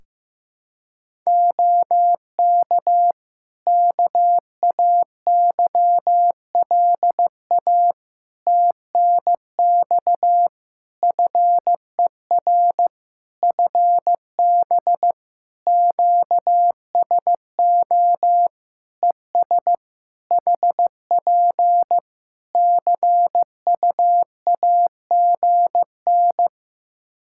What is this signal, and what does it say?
Morse code, heard as a keyed tone, about 16 wpm, 700 Hz. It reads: OK KAYLA TNX FER FB QSO ES HP CUAGN
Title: Practice QSO - Exchange 4 - Ending - 15wpm